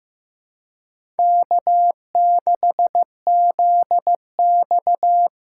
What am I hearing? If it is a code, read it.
Morse code, heard as a keyed tone, about 15 wpm, 700 Hz. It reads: K6ZX
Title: K1USN SST QSO Practice - 15wpm